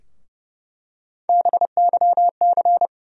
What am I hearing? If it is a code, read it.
Morse code, heard as a keyed tone, about 30 wpm, 700 Hz. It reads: BYC
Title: USA Call Sign Postfixes - 3 Letters 30wpm